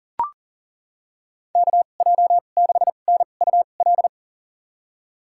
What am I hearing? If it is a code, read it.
Morse code, heard as a keyed tone, about 40 wpm, spaced as for 29 wpm, 700 Hz. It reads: KJ6NUL
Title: USA Call Signs - 6 Characters - 2x Character Spacing - 40wpm